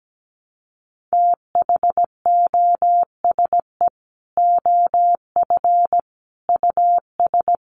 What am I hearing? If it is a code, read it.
Morse code, heard as a keyed tone, about 17 wpm, 700 Hz. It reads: THOSE OF US
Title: Sets of 5 Words Encore2 17wpm